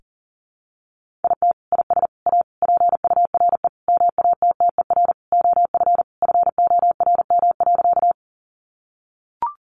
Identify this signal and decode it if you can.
Morse code, heard as a keyed tone, about 40 wpm, 700 Hz. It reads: IT IS A PURE MATTER OF FORM.
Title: Sentences from Top 1500 Words 40wpm